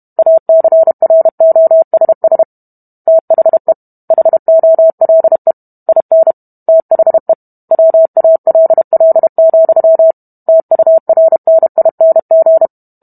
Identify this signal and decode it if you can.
Morse code, heard as a keyed tone, about 32 wpm, 650 Hz. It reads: ACROSS THE HOLE IN THE WALL, TURNING